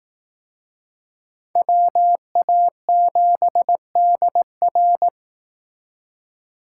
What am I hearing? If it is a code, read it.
Morse code, heard as a keyed tone, about 19 wpm, 700 Hz. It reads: WA7DR
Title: Speed Racing - USA 2x2 Call Signs 15wpm